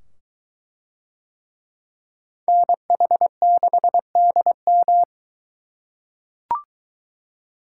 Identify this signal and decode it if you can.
Morse code, heard as a keyed tone, about 22 wpm, 700 Hz. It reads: NH6DM